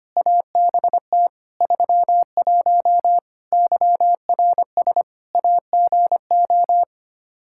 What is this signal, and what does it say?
Morse code, heard as a keyed tone, about 25 wpm, 700 Hz. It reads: ABT 31 YRS AGO
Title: K1USN SST QSO Practice 25wpm